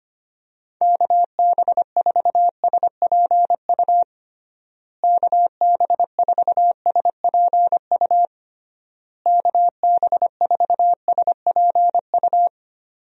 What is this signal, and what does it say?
Morse code, heard as a keyed tone, about 25 wpm, 700 Hz. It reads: KB4SPU KB4SPU KB4SPU
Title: USA Call Signs - 6 Characters - Repeated 3x - 25wpm